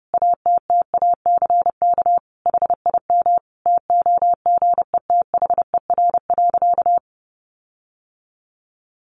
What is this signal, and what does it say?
Morse code, heard as a keyed tone, about 30 wpm, 700 Hz. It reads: ATTACK HIM TOGETHER.
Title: Sentences from Top 1500 Words 30wpm